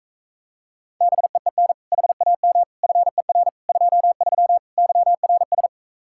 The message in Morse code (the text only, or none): BEEN HAM FER 23 YRS